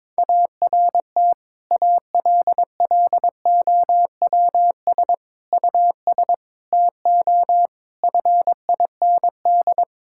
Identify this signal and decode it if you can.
Morse code, heard as a keyed tone, about 22 wpm, 700 Hz. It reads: ART ALLOWS US TO FIND
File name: Difficult Sentences with Hints - 22wpm